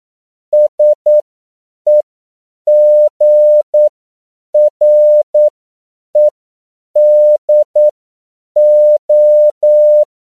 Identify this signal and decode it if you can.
Morse code, heard as a keyed tone, about 9 wpm, spaced as for 7 wpm, 600 Hz. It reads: SEGREDO
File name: CW / 10PPM